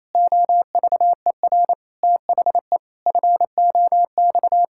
Morse code, heard as a keyed tone, about 28 wpm, 700 Hz. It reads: OVER THE FOX
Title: Pangram Challenge v2 - 28wpm